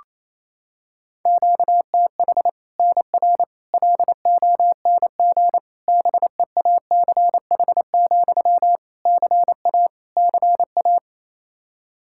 Morse code, heard as a keyed tone, about 28 wpm, 700 Hz. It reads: QTH NR LONG BEACH, CA CA
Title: K1USN SST QSO Practice 28wpm